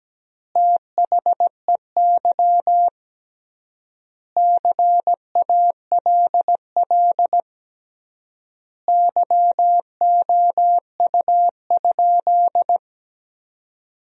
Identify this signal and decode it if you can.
Morse code, heard as a keyed tone, about 17 wpm, 700 Hz. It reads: THEY CALL YOU?
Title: 3x Word-Spacing - Sentences from Top 100 Words - 17wpm